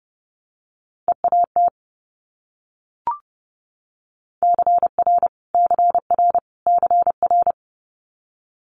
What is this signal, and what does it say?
Morse code, heard as a keyed tone, about 31 wpm, 690 Hz. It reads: EAT CR CR CR
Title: Multi-letter Phonemes Repeated 3x 30wpm